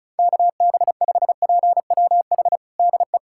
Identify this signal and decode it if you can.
Morse code, heard as a keyed tone, about 36 wpm, 700 Hz. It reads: KB5PWH DE
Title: USA Callsign DE Callsign 35wpm